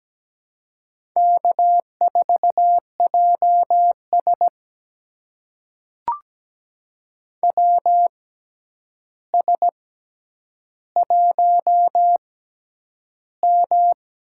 Morse code, heard as a keyed tone, about 17 wpm, spaced as for 9 wpm, 700 Hz. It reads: K4JS WS1M
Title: USA Call Signs - 4 Characters - 6x Character Spacing - 17wpm